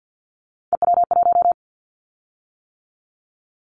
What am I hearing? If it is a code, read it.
Morse code, heard as a keyed tone, about 48 wpm, 700 Hz. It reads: EW1
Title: Callsign Prefixes for Top 100 DXCC - 50wpm